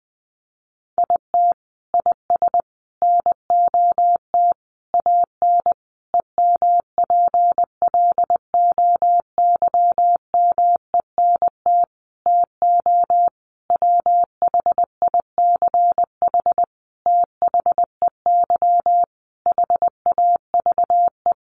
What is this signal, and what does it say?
Morse code, heard as a keyed tone, about 20 wpm, 700 Hz. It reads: IT IS NOT AN EMPLOYMENT TO WHICH THEY HAVE
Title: Sentences from Top 1500 Words 20wpm